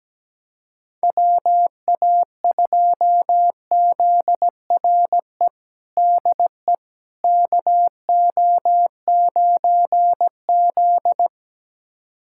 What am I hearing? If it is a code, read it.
Morse code, heard as a keyed tone, about 17 wpm, 700 Hz. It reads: WA2ZRE DE KO9Z
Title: USA Callsign DE Callsign 17wpm